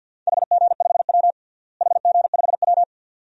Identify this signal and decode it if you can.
Morse code, heard as a keyed tone, about 51 wpm, 700 Hz. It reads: HG5W HG5W
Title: DXCC-100 Call Signs - 4 Characters - Repeated 2x - 50wpm